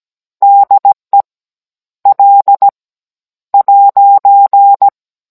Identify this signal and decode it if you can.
Morse code, heard as a keyed tone, about 17 wpm, 800 Hz. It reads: DE L '
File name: Cours-NR19-16mm